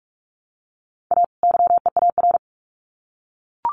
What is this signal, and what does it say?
Morse code, heard as a keyed tone, about 44 wpm, 700 Hz. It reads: A YEAR
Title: Sets of 2 Words 45wpm